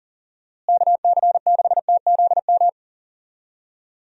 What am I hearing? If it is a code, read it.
Morse code, heard as a keyed tone, about 39 wpm, 700 Hz. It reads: KC6TZM
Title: USA Call Signs - 6 Characters - Repeated 2x - 40wpm